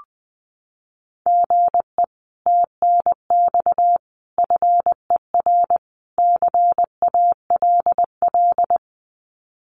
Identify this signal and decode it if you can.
Morse code, heard as a keyed tone, about 20 wpm, 700 Hz. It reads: GE TNX FER CALL
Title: Common QSO Phrases Encore 20wpm